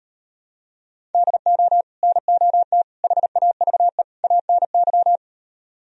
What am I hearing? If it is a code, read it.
Morse code, heard as a keyed tone, about 38 wpm, 700 Hz. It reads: DO NOT HAVE ANY